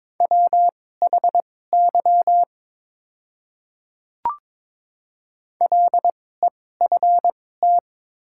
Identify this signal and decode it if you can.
Morse code, heard as a keyed tone, about 22 wpm, spaced as for 16 wpm, 700 Hz. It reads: WHY LEFT
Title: 2x Character Spacing - Top 300 Words - 22wpm